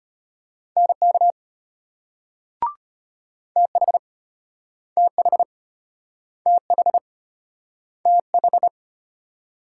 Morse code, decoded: NK TH TH TH TH

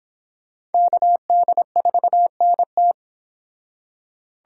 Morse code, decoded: KD4NT

